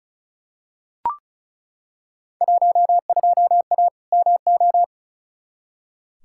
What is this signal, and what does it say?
Morse code, heard as a keyed tone, about 36 wpm, 690 Hz. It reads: 12A MO
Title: Field Day Exchanges Encore - 35wpm